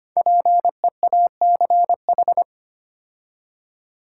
Morse code, heard as a keyed tone, about 25 wpm, 700 Hz. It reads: PEACH
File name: practice / 25wpm / Difficult Words with Hints - 25wpm